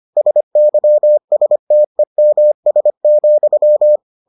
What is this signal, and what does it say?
Morse code, heard as a keyed tone, about 25 wpm, 600 Hz. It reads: SYSTEMS,